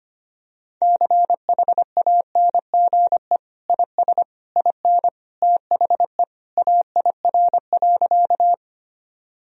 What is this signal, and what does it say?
Morse code, heard as a keyed tone, about 25 wpm, 700 Hz. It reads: CHANGE IS IN THE AIR.
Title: Sentences from Top 1000 Words - Review Part 1 - 25wpm